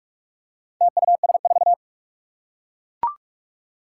Morse code, heard as a keyed tone, about 44 wpm, 700 Hz. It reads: TUS4
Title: Four Challenging Letter-Number Combinations 45wpm